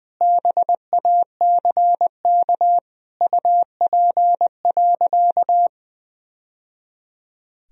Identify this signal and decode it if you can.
Morse code, heard as a keyed tone, about 21 wpm, 700 Hz. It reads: BACK UP.
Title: Sentences from Top 1000 Words - Review Part 2 - 20wpm